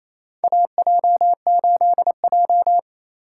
Morse code, heard as a keyed tone, about 28 wpm, 700 Hz. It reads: AJ8J
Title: USA Call Signs - 4 Characters - 8x Character Spacing - 28wpm